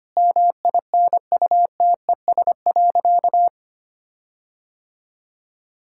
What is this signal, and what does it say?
Morse code, heard as a keyed tone, about 25 wpm, 700 Hz. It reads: MINUTES.
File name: practice / 25wpm / Difficult Sentences with Hints - 25wpm